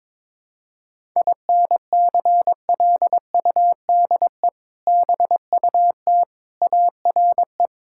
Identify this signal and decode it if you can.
Morse code, heard as a keyed tone, about 22 wpm, 700 Hz. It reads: INCLUDE BUT ARE
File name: practice / 22wpm / Sets of 6 Words - 22wpm